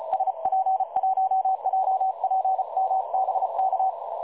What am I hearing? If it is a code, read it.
Morse code, heard as a keyed tone, about 34 wpm, 760 Hz. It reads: SP1JPM BK
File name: dlazaawansowanych / nagraniazpasma / websdr recording 3565.0kHz